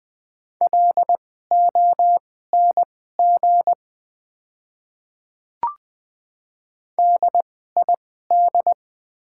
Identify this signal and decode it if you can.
Morse code, heard as a keyed tone, about 20 wpm, spaced as for 15 wpm, 700 Hz. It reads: LONG DID
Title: 2x Character Spacing - Top 100 Words - 20wpm